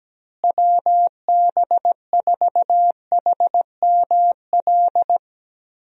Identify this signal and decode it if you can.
Morse code, heard as a keyed tone, about 17 wpm, 700 Hz. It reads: WB4HML
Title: USA Call Signs - 6 Characters - Repeated 2x - 17wpm